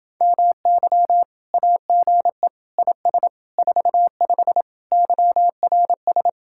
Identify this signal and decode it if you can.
Morse code, heard as a keyed tone, about 30 wpm, 700 Hz. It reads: MY AGE IS 45 YRS